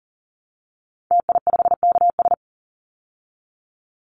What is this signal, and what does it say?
Morse code, heard as a keyed tone, about 37 wpm, 700 Hz. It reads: TI5KS
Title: Call Signs Repeated 2x - 40wpm